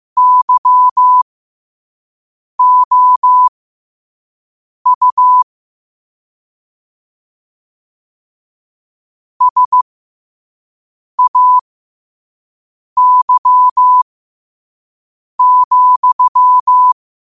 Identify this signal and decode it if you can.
Morse code, heard as a keyed tone, about 15 wpm, spaced as for 5 wpm, 1000 Hz. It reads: YOU SAY,
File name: qotdfiles / 5wpm